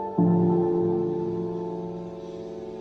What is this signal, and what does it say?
Dial tone (North America), 350 + 440 Hz.